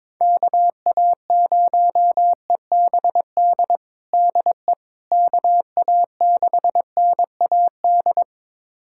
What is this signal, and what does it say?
Morse code, heard as a keyed tone, about 22 wpm, 700 Hz. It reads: KA0EBD DE KA6NAD
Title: USA Callsign DE Callsign - 6 Characters - 22wpm